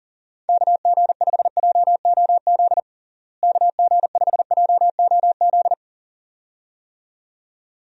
Morse code, heard as a keyed tone, about 40 wpm, 700 Hz. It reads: KG5JOZ KG5JOZ
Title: USA Call Signs - 6 Characters - Repeated 2x - 40wpm